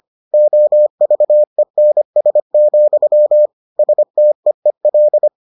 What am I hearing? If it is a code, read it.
Morse code, heard as a keyed tone, about 25 wpm, 600 Hz. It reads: OVENS, STEEL